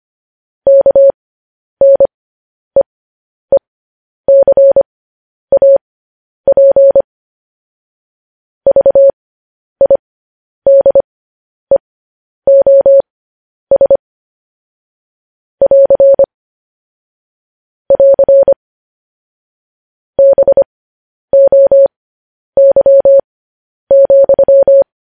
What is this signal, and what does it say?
Morse code, heard as a keyed tone, about 25 wpm, spaced as for 10 wpm, 550 Hz. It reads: KNEECAP VIDEOS <AR> <AR> BOY,
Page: news-headlines-in-morse-code-at-10-wpm